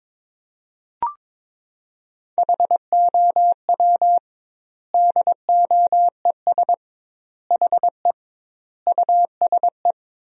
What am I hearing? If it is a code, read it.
Morse code, heard as a keyed tone, about 22 wpm, 700 Hz. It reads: HOW DOES HE USE
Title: 2x Word-Spacing - Sentences from Top 200 Words - 22wpm